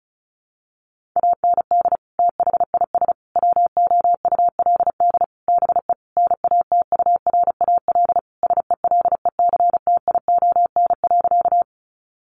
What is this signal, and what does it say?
Morse code, heard as a keyed tone, about 34 wpm, 700 Hz. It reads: AND THIS WOULD BE NATURAL SELECTION.
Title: Sentences from Top 1400 Words 35wpm